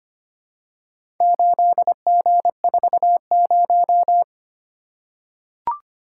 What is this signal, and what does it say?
Morse code, heard as a keyed tone, about 25 wpm, 700 Hz. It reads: 8G40
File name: practice / 25wpm / Four Letter-Number Combinations - 25wpm